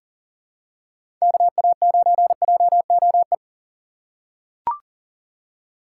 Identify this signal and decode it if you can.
Morse code, heard as a keyed tone, about 40 wpm, 700 Hz. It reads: KA9JOE
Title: USA Call Signs - 6 Characters - 4x Character Spacing - 40wpm